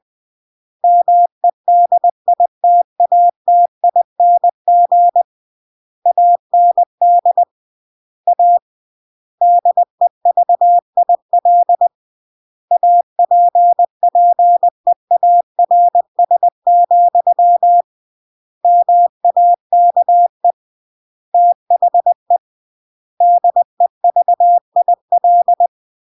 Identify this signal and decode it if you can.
Morse code, heard as a keyed tone, about 20 wpm, 700 Hz. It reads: MEDITATING AND A DEVIL APPEARS, MAKE THE DEVIL
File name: CW / jodorowsky / 82 commandments 20-20